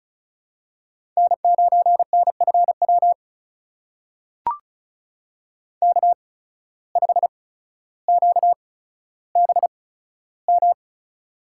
Morse code, heard as a keyed tone, about 36 wpm, spaced as for 15 wpm, 700 Hz. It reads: N9NFW K5QBM
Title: USA Call Signs - 5 Characters - 8x Character Spacing - 35wpm